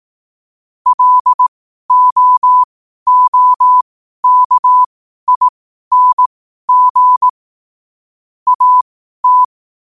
Morse code, heard as a keyed tone, about 18 wpm, spaced as for 12 wpm, 1000 Hz. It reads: LOOKING AT
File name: qotdfiles / 12wpm